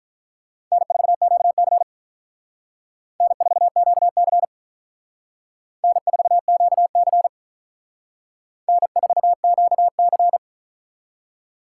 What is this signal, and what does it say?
Morse code, heard as a keyed tone, about 44 wpm, 700 Hz. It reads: N4QC N4QC N4QC N4QC